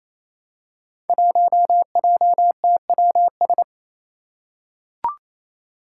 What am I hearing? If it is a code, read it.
Morse code, heard as a keyed tone, about 29 wpm, 700 Hz. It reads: 1JTWS
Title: Five Challenging Letter-Number Combinations 28wpm